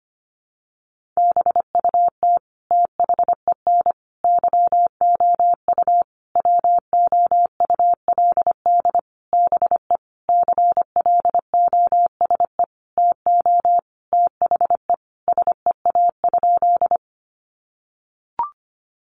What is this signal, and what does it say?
Morse code, heard as a keyed tone, about 25 wpm, 700 Hz. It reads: BUT THEN YOU WOULD BE CLOSE TO THE SEA?
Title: Sentences from Top 300 Words 25wpm